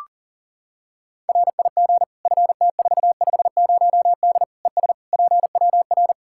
Morse code, heard as a keyed tone, about 40 wpm, 700 Hz. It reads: RIG FT450D ES PWR
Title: Practice QSO - Exchange 2 - Info - 40wpm